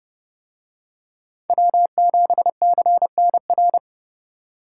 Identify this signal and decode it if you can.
Morse code, heard as a keyed tone, about 30 wpm, 700 Hz. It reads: W7CNR